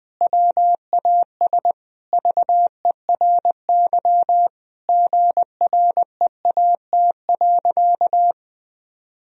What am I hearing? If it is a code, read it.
Morse code, heard as a keyed tone, about 20 wpm, 700 Hz. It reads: WAS VERY GREAT.
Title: Sentences from Top 1000 Words - Review Part 1 - 20wpm